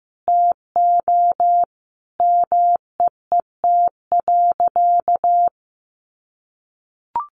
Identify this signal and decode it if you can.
Morse code, heard as a keyed tone, about 14 wpm, 700 Hz. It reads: TO MEET.
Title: Sentences from Top 1200 Words 15wpm